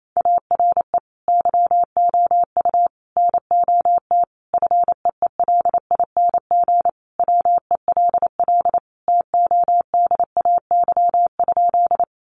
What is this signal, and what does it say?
Morse code, heard as a keyed tone, about 28 wpm, 700 Hz. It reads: ARE YOU NOT FEELING WELL TODAY?
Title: Sentences from Top 1200 Words 28wpm